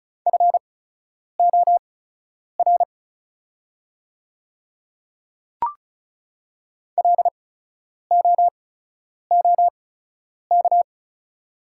Morse code, heard as a keyed tone, about 36 wpm, spaced as for 10 wpm, 700 Hz. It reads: FOR LOOK
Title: 8x Character Spacing - Top 100 Words - 35wpm